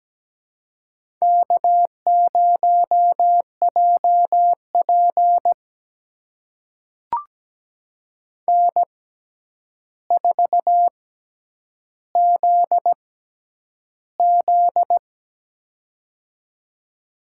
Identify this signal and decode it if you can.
Morse code, heard as a keyed tone, about 17 wpm, spaced as for 9 wpm, 700 Hz. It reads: K0JP N4ZZ
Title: USA Call Signs - 4 Characters - 6x Character Spacing - 17wpm